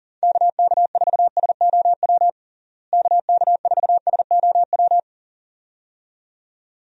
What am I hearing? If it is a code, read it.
Morse code, heard as a keyed tone, about 40 wpm, 700 Hz. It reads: KK4SOW KK4SOW
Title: USA Call Signs - 6 Characters - Repeated 2x - 40wpm